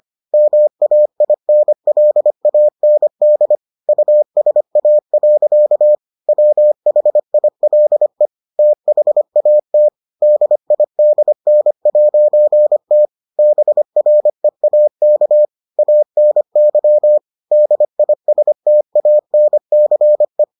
Morse code, heard as a keyed tone, about 25 wpm, 600 Hz. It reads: MAINLAND USA. WHILE THAT DIDN'T BREAK ANY DISTANCE